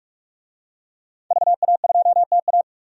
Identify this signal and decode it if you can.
Morse code, heard as a keyed tone, about 42 wpm, 700 Hz. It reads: VA2TA
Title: DXCC-100 Call Signs - 5 Characters - 45wpm